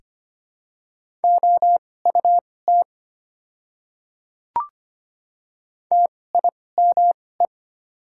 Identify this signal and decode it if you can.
Morse code, heard as a keyed tone, about 25 wpm, spaced as for 18 wpm, 690 Hz. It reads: OUT TIME
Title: 2x Character Spacing - Top 100 Words - 25wpm